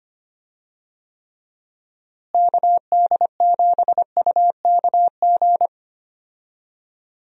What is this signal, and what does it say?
Morse code, heard as a keyed tone, about 25 wpm, 700 Hz. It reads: KD7UKG
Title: USA Call Signs - 6 Characters - Repeated 2x - 25wpm